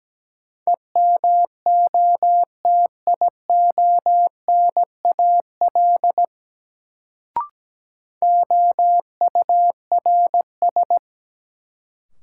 Morse code, heard as a keyed tone, about 17 wpm, 700 Hz. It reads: EMOTIONAL OURS
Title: Sentences from Top 1000 Words - Review Part 5 - New Word Drill - 17wpm